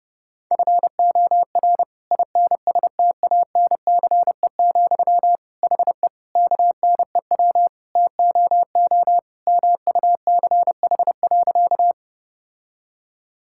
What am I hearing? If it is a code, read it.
Morse code, heard as a keyed tone, about 32 wpm, 700 Hz. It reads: FOR INSTANCE, HE KNEW TOO MUCH.
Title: Sentences from Top 1000 Words - Review Part 5 - 30wpm